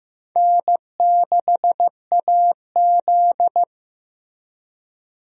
Morse code, heard as a keyed tone, about 15 wpm, 700 Hz. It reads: N6AZ
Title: USA Call Signs - 4 Characters - Repeated 2x - 15wpm